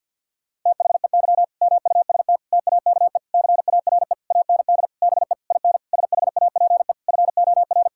Morse code, heard as a keyed tone, about 50 wpm, 700 Hz. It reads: THEY MUST TAKE CARE AND BE IN SHAPE FOR
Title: Sentences from Top 1000 Words - Review Part 6 - 50wpm